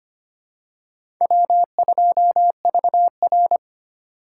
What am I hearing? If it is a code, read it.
Morse code, heard as a keyed tone, about 25 wpm, 700 Hz. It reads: W2VR
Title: USA Call Signs - 4x Character Spacing - 25wpm